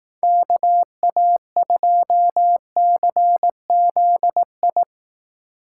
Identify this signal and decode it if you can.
Morse code, heard as a keyed tone, about 18 wpm, 700 Hz. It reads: KA2CZI
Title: Speed Racing - USA 2x3 Call Signs 15wpm